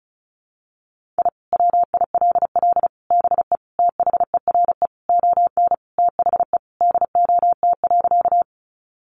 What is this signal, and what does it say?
Morse code, heard as a keyed tone, about 34 wpm, 700 Hz. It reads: I WILL BE THERE ON THE DOT.
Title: Sentences from Top 1500 Words 35wpm